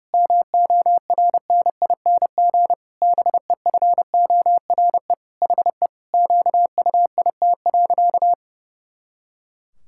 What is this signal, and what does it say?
Morse code, heard as a keyed tone, about 30 wpm, 700 Hz. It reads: MORNING BEFORE HE QUIT.
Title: Pangram Challenge v2 - 30wpm